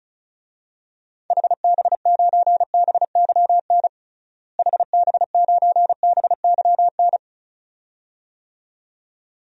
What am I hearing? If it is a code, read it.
Morse code, heard as a keyed tone, about 36 wpm, 700 Hz. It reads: HB9BYN HB9BYN
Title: DXCC-100 Call Signs - 6 Characters - Repeated 2x - 35wpm